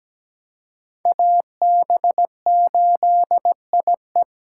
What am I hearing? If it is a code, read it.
Morse code, heard as a keyed tone, about 20 wpm, 700 Hz. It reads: AB8IE
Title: Speed Racing - USA 2x2 Call Signs 17wpm